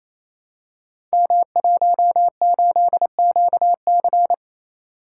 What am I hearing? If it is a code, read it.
Morse code, heard as a keyed tone, about 29 wpm, 700 Hz. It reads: M18QC
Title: Code Groups - 5 Letter:Numbers - 28wpm